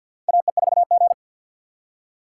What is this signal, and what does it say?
Morse code, heard as a keyed tone, about 53 wpm, 700 Hz. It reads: AE4G
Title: USA Call Signs - 4 Characters - 3x Character Spacing - 50wpm